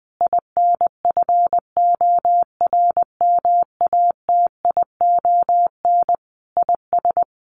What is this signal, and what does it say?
Morse code, heard as a keyed tone, about 20 wpm, 700 Hz. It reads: INFORMATION IS
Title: Sentences from Top 1100 Words 20wpm